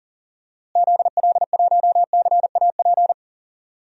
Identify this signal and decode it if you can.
Morse code, heard as a keyed tone, about 40 wpm, 700 Hz. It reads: ZL1CAP
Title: DXCC-100 Call Signs - 6 Characters - Repeated 2x - 40wpm